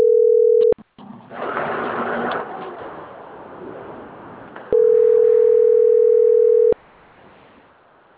Ringback tone (North America), 440 + 480 Hz.